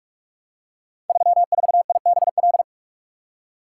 Morse code, heard as a keyed tone, about 46 wpm, 700 Hz. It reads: 34IBL
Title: Code Groups - 5 Letter:Numbers - 45wpm